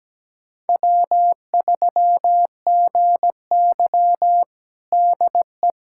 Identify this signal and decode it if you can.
Morse code, heard as a keyed tone, about 17 wpm, 700 Hz. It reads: W3GY DE
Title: USA Callsign DE Callsign - 4 Characters - 17wpm